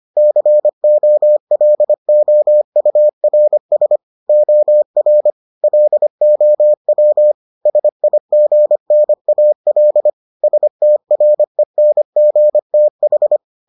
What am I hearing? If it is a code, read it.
Morse code, heard as a keyed tone, about 25 wpm, 600 Hz. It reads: COLOURS OR LOW SIGNAL STRENGTH